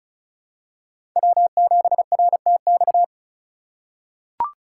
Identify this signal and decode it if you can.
Morse code, heard as a keyed tone, about 34 wpm, 700 Hz. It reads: W7RTX
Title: USA Call Signs - 5 Characters - 2x Character Spacing - 35wpm